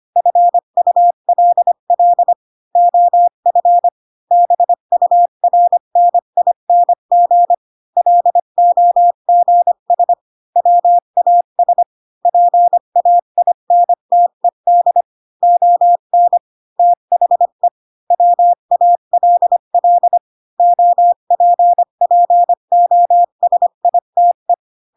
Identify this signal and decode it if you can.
Morse code, heard as a keyed tone, about 25 wpm, 700 Hz. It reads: FULL OF BURNING LOGS WAS PAINTED ON THE WALL OPPOSITE